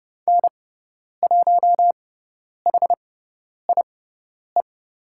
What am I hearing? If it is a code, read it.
Morse code, heard as a keyed tone, about 30 wpm, spaced as for 10 wpm, 700 Hz. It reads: N1HIE
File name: Lesson 18g - Letter m - Call Signs 30wpm